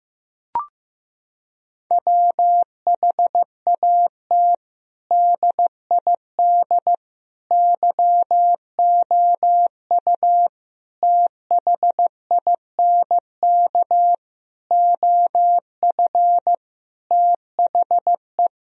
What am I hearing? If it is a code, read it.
Morse code, heard as a keyed tone, about 15 wpm, 700 Hz. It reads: WHAT DID YOU THINK OF THE
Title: Sentences from Top 1000 Words - Review Part 4 - 15wpm